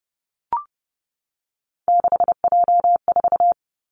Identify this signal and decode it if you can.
Morse code, heard as a keyed tone, about 30 wpm, 700 Hz. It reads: 6J4
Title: Three Challenging Letter-Number Combinations 30wpm